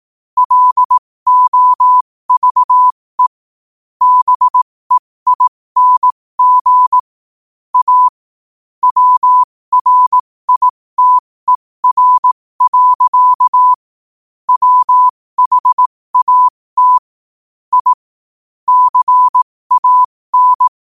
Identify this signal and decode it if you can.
Morse code, heard as a keyed tone, about 18 wpm, spaced as for 15 wpm, 1000 Hz. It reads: LOVE BEING A WRITER. WHAT I CAN
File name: qotdfiles / 15wpm